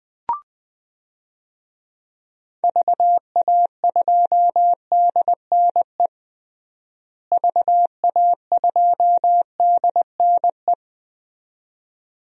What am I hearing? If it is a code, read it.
Morse code, heard as a keyed tone, about 20 wpm, 700 Hz. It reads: VA2DNE VA2DNE
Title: DXCC-100 Call Signs - 6 Characters - Repeated 2x - 20wpm